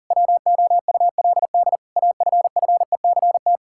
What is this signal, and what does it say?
Morse code, heard as a keyed tone, about 40 wpm, 700 Hz. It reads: WOULD AFFECT
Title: Sentences from Top 1000 Words - Review Part 6 - 40wpm